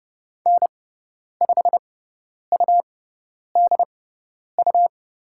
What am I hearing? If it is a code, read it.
Morse code, heard as a keyed tone, about 30 wpm, spaced as for 10 wpm, 700 Hz. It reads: N5UDU
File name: Lesson 17g - Letter u - Call Signs 30wpm